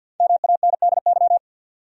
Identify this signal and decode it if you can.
Morse code, heard as a keyed tone, about 46 wpm, 700 Hz. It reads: MANDY
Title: USA Common Names - 50wpm